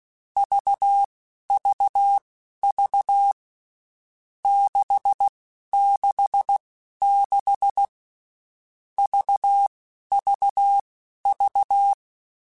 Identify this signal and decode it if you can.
Morse code, heard as a keyed tone, about 16 wpm, spaced as for 11 wpm, 800 Hz. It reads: VVV 666 VVV
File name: Groups@10 / Groups@10 mp3s / VB46@10